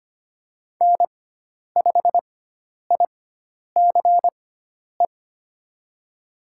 Morse code, heard as a keyed tone, about 25 wpm, spaced as for 10 wpm, 700 Hz. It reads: N5ICE
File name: Lesson 16g - Letter c - Call Signs 25wpm